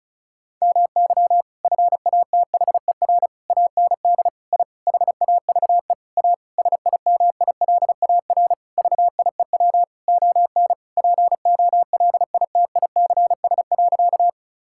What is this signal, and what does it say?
Morse code, heard as a keyed tone, about 34 wpm, 700 Hz. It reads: MY FATHER AND I HAVE A SIMILAR VIEW ON POLITICS.
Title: Sentences from Top 1000 Words - Review Part 6 - 35wpm